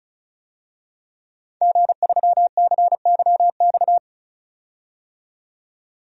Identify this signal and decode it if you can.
Morse code, heard as a keyed tone, about 34 wpm, 700 Hz. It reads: G3CYX
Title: DXCC-100 Call Signs - 5 Characters - 35wpm